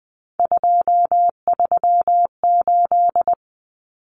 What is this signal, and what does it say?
Morse code, heard as a keyed tone, about 20 wpm, 700 Hz. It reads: 238
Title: Numbers - 3-Digit 20wpm